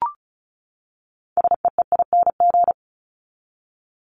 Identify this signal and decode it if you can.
Morse code, heard as a keyed tone, about 34 wpm, 700 Hz. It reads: SEEING
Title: ING Suffix - Top 100 Words with Base Words 35wpm